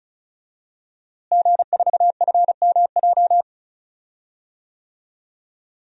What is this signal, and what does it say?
Morse code, heard as a keyed tone, about 34 wpm, 700 Hz. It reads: G4FMJ
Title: DXCC-100 Call Signs - 5 Characters - Repeated 3x - 35wpm